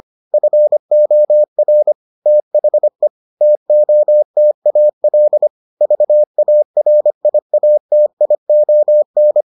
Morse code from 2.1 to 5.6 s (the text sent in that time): THE TOTAL